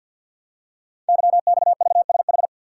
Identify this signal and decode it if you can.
Morse code, heard as a keyed tone, about 50 wpm, 700 Hz. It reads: YXVSH
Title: Code Groups - 5 Letter:Numbers - 50wpm